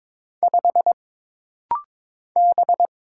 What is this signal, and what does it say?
Morse code, heard as a keyed tone, about 22 wpm, 700 Hz. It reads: 5 B
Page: This rapid-fire timing is geared to push your ICR (Instant Character Recognition) skills.